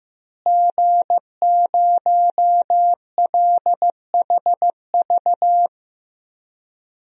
Morse code, heard as a keyed tone, about 15 wpm, 700 Hz. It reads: G0LHV
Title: DXCC-100 Call Signs - 5 Characters - Repeated 3x - 15wpm